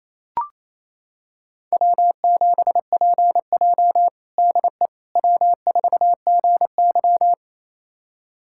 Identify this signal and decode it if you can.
Morse code, heard as a keyed tone, about 28 wpm, 700 Hz. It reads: W7PJ DE W4GY
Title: USA Callsign DE Callsign - 4 Characters - 28wpm